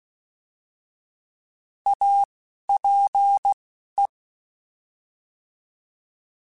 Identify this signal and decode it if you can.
Morse code, heard as a keyed tone, about 16 wpm, 800 Hz. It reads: A P E